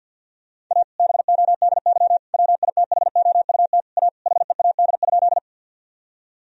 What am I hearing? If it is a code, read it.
Morse code, heard as a keyed tone, about 50 wpm, 700 Hz. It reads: A BODY WITHOUT A HEAD?
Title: Sentences from Top 1000 Words - Review Part 3 - 50wpm